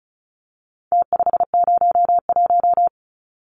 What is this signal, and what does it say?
Morse code, heard as a keyed tone, about 34 wpm, 700 Hz. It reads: T501
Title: Numbers - Contest Style - 4-Digit 35wpm